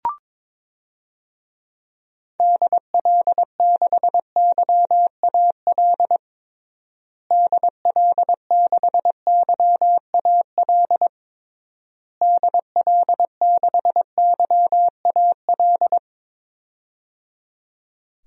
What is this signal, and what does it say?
Morse code, heard as a keyed tone, about 22 wpm, 700 Hz. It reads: DL6YAL DL6YAL DL6YAL
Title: DXCC-100 Call Signs - 6 Characters - Repeated 3x - 22wpm